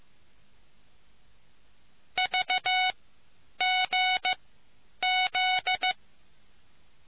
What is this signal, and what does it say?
Morse code, heard as a keyed tone, about 15 wpm, spaced as for 9 wpm, 740 Hz. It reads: VGZ